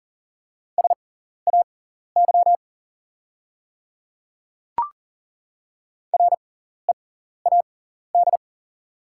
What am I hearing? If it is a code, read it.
Morse code, heard as a keyed tone, about 39 wpm, spaced as for 14 wpm, 700 Hz. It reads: SAY READ
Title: 6x Character Spacing - Top 200 Words - 40wpm